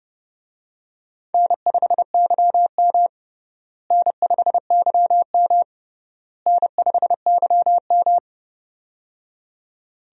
Morse code, heard as a keyed tone, about 31 wpm, 700 Hz. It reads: N5YM N5YM N5YM
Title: USA Call Signs - 4 Characters - Repeated 3x - 30wpm